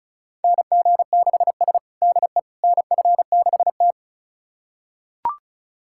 Morse code, heard as a keyed tone, about 36 wpm, 700 Hz. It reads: NG6S DE NF6T
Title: USA Callsign DE Callsign - 4 Characters - 35wpm